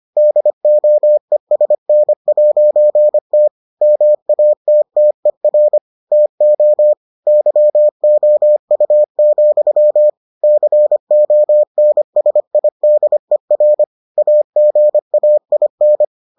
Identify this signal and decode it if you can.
Morse code, heard as a keyed tone, about 25 wpm, 600 Hz. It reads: DOESN'T MATTER TO YOU, CONSIDER AGAIN